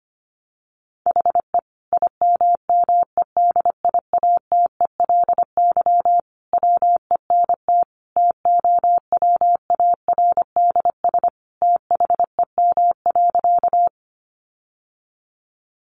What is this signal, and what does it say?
Morse code, heard as a keyed tone, about 25 wpm, 700 Hz. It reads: HE IMMEDIATELY WENT TOWARDS THEM.
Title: Sentences from Top 1100 Words 25wpm